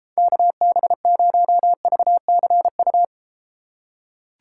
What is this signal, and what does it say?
Morse code, heard as a keyed tone, about 33 wpm, 700 Hz. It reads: KB0VCU